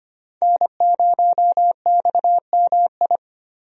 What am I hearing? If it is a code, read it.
Morse code, heard as a keyed tone, about 25 wpm, 700 Hz. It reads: N0XMI
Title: USA Call Signs - 5 Characters - Repeated 2x - 25wpm